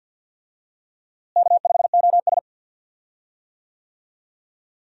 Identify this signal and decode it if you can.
Morse code, heard as a keyed tone, about 51 wpm, 700 Hz. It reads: K5OS
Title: USA Call Signs - 4 Characters - Repeated 3x - 50wpm